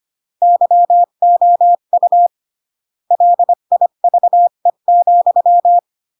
Morse code, heard as a keyed tone, about 25 wpm, 700 Hz. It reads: YOU LIVE,